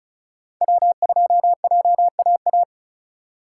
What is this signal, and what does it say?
Morse code, heard as a keyed tone, about 36 wpm, 700 Hz. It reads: W2JAA
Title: USA Call Signs - 5 Characters - 2x Character Spacing - 35wpm